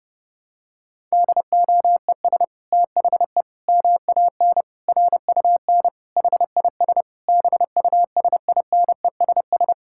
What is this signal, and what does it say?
Morse code, heard as a keyed tone, about 32 wpm, 700 Hz. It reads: DOES THE MAN RUN HIS BUSINESS
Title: Sentences from Top 1000 Words - Review Part 5 - 30wpm